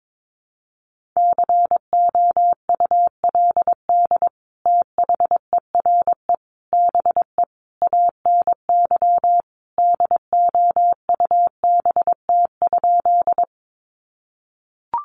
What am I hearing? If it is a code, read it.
Morse code, heard as a keyed tone, about 22 wpm, 700 Hz. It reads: COULD THERE BE ANY DOUBT?
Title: Sentences from Top 1500 Words 22wpm